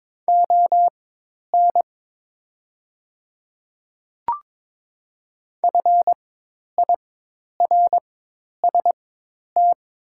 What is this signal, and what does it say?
Morse code, heard as a keyed tone, about 22 wpm, spaced as for 10 wpm, 700 Hz. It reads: ON FIRST
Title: 4x Character Spacing - Top 100 Words - 22wpm